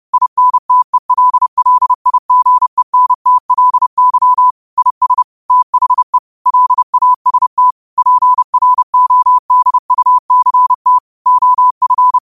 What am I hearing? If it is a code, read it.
Morse code, heard as a keyed tone, about 30 wpm, 1000 Hz. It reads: INTELLIGENTLY IS THE LAST PRODUCT OF
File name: qotdfiles / 30wpm